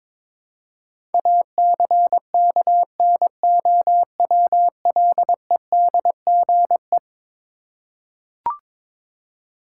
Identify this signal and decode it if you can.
Morse code, heard as a keyed tone, about 22 wpm, 700 Hz. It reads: ACKNOWLEDGE
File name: practice / 22wpm / Eleven Letter Words - 22wpm